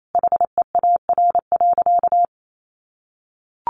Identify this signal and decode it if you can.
Morse code, heard as a keyed tone, about 28 wpm, 700 Hz. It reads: HEAR.
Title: Sentences from Top 600 Words 28wpm